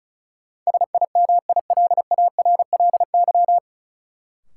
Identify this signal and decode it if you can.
Morse code, heard as a keyed tone, about 36 wpm, 700 Hz. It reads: SIMILARLY
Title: Sentences from Top 1400 Words - New Word Drill Alt - 35wpm